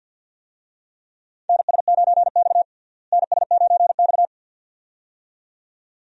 Morse code, heard as a keyed tone, about 52 wpm, 700 Hz. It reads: NS9X NS9X
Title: USA Call Signs - 4 Characters - Repeated 2x - 50wpm